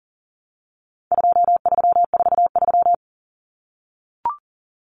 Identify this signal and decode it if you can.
Morse code, heard as a keyed tone, about 40 wpm, 700 Hz. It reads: 2343
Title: Numbers - 4-Digit 40wpm